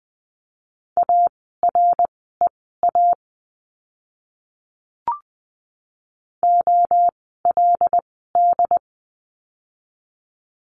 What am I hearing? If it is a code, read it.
Morse code, heard as a keyed tone, about 20 wpm, spaced as for 15 wpm, 700 Hz. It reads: AREA OLD
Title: 115 Lesson 19f - Letter w - Words 20wpm